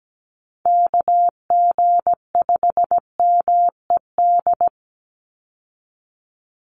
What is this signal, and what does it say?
Morse code, heard as a keyed tone, about 17 wpm, 700 Hz. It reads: KG5MED
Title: USA Call Signs 17wpm